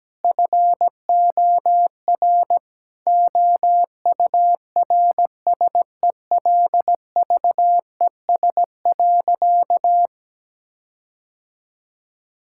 Morse code, heard as a keyed tone, about 17 wpm, 700 Hz. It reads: FOR OURSELVES.
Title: Sentences from Top 1600 Words 17wpm